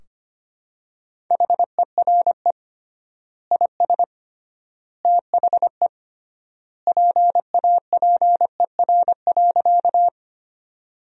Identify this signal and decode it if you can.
Morse code, heard as a keyed tone, about 25 wpm, 700 Hz. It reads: HERE IS THE PAPER.
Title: 3x Word-Spacing - Sentences from Top 300 Words - 25wpm